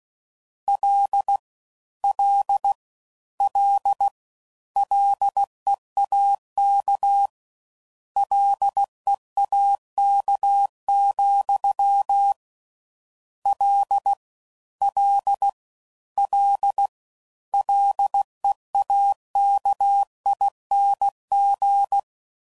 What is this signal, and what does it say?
Morse code, heard as a keyed tone, about 16 wpm, 800 Hz. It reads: L L L LEAK LEAK, L L L LEAKING